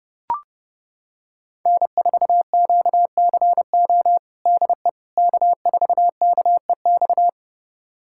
Morse code, heard as a keyed tone, about 32 wpm, 700 Hz. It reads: N4QCO DE K4KEX
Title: USA Callsign DE Callsign 30wpm